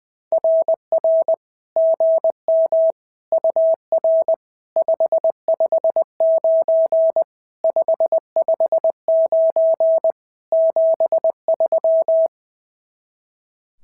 Morse code, heard as a keyed tone, about 20 wpm, 650 Hz. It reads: RR GM UR 559 559 73
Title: SOTA QSO Practice 20wpm